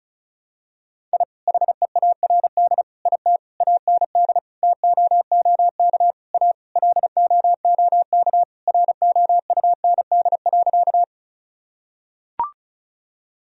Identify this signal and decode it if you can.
Morse code, heard as a keyed tone, about 34 wpm, 700 Hz. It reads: I HEARD IT AND TOOK A LOOK ROUND.
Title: Sentences from Top 1000 Words - Review Part 1 - 35wpm